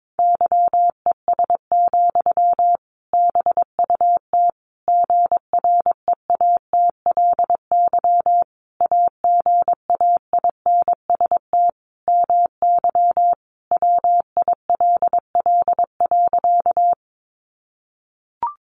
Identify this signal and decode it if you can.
Morse code, heard as a keyed tone, about 22 wpm, 700 Hz. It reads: YES, BUT GREATLY AGAINST MY WILL.
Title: Sentences from Top 1400 Words 22wpm